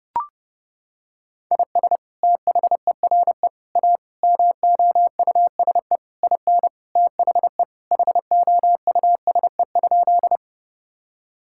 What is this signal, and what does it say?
Morse code, heard as a keyed tone, about 32 wpm, 700 Hz. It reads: IS THERE A MOUSE IN THE HOUSE?
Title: Sentences from Top 1000 Words - Review Part 5 - 30wpm